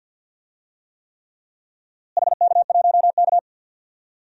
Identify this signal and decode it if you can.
Morse code, heard as a keyed tone, about 47 wpm, 700 Hz. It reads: HK1K
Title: DXCC-100 Call Signs - 4 Characters - Repeated 3x - 50wpm